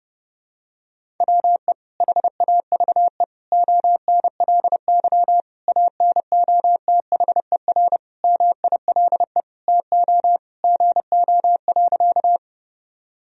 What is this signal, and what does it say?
Morse code, heard as a keyed tone, about 30 wpm, 700 Hz. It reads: WE HAVE ONLY ANOTHER MILE TO GO.
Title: Sentences from Top 1000 Words - Review Part 1 - 30wpm